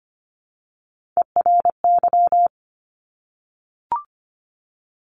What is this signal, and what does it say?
Morse code, heard as a keyed tone, about 25 wpm, 700 Hz. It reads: ERY
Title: Top Three Letter Combinations in Words 25wpm